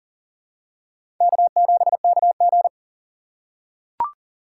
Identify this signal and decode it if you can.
Morse code, heard as a keyed tone, about 40 wpm, 700 Hz. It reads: K7KG
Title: USA Call Signs - 6x Character Spacing - 40wpm